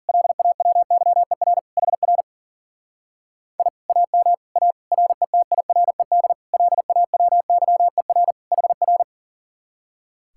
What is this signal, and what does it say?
Morse code, heard as a keyed tone, about 48 wpm, 700 Hz. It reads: LAWYER HR I AM A RETIRED LAWYER HR
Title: Speed-Racing Common QSO Phrases - 40wpm